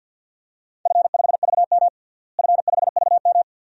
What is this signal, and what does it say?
Morse code, heard as a keyed tone, about 50 wpm, 700 Hz. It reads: F5VM F5VM
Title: DXCC-100 Call Signs - 4 Characters - Repeated 2x - 50wpm